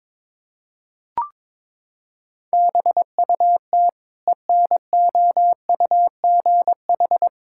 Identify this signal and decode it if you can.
Morse code, heard as a keyed tone, about 21 wpm, 700 Hz. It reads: BUT ENOUGH
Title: Sentences from Top 1000 Words - Review Part 5 - 22wpm